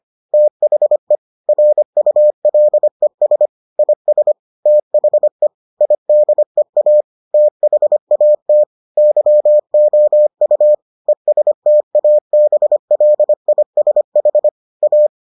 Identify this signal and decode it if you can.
Morse code, heard as a keyed tone, about 25 wpm, 600 Hz. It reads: THE RULES IS THE IDEA THAT YOU ESTABLISH A